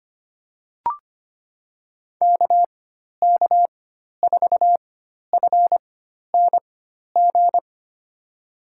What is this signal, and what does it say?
Morse code, heard as a keyed tone, about 25 wpm, spaced as for 12 wpm, 700 Hz. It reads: KK4FNG
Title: USA Call Signs - 4x Character Spacing - 25wpm